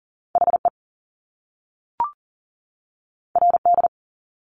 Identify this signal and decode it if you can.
Morse code, heard as a keyed tone, about 40 wpm, 700 Hz. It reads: HE RD
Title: Top Two Letter Combinations in Words 40wpm